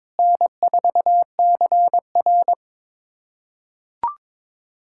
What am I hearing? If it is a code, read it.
Morse code, heard as a keyed tone, about 22 wpm, 700 Hz. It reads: N4CR
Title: USA Callsign DE Callsign - 4 Characters - 22wpm